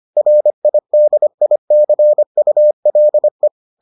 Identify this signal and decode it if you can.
Morse code, heard as a keyed tone, about 25 wpm, 600 Hz. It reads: RIDICULE